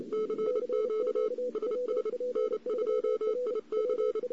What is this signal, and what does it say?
Morse code, heard as a keyed tone, about 29 wpm, 430 Hz. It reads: CQ SS N2IC
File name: CW / 2008-ss-cw-k5tr-op-n5rz -02:00